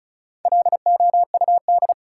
Morse code, heard as a keyed tone, about 36 wpm, 700 Hz. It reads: LOUD
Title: Four Letter Words - 35wpm